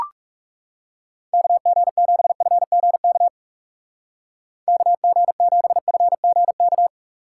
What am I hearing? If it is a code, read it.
Morse code, heard as a keyed tone, about 37 wpm, 700 Hz. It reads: KG7FGK KG7FGK
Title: Speed Racing - USA 2x3 Call Signs 30wpm